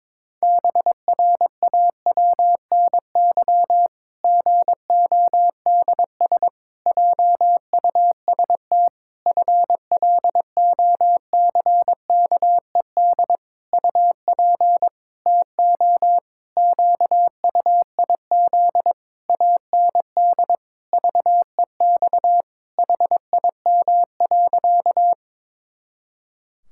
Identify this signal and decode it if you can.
Morse code, heard as a keyed tone, about 22 wpm, 700 Hz. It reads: BRAWNY GODS JUST FLOCKED UP TO QUIZ AND VEX HIM.
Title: Pangram Challenge v2 - 22wpm